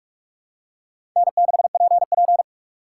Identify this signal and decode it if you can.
Morse code, heard as a keyed tone, about 46 wpm, 700 Hz. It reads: N6PP
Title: USA Call Signs - 4 Characters - 6x Character Spacing - 45wpm